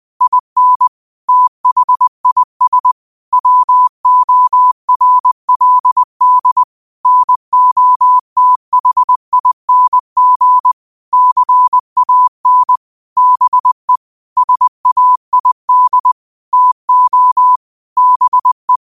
Morse code, heard as a keyed tone, about 20 wpm, 1000 Hz. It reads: IN THIS WORLD NOTHING CAN BE SAID TO BE
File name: qotdfiles / 20wpm